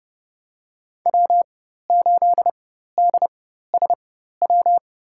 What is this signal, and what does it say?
Morse code, heard as a keyed tone, about 30 wpm, spaced as for 14 wpm, 700 Hz. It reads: W8DSW
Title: USA Call Signs - 5 Characters - 4x Character Spacing - 30wpm